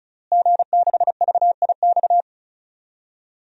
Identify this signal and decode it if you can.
Morse code, heard as a keyed tone, about 34 wpm, 700 Hz. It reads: G6VIX
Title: DXCC-100 Call Signs - 5 Characters - Repeated 2x - 35wpm